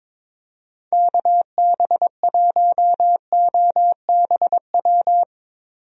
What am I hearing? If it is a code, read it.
Morse code, heard as a keyed tone, about 22 wpm, 700 Hz. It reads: KB1OBW
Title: USA Call Signs - 6 Characters - Repeated 3x - 22wpm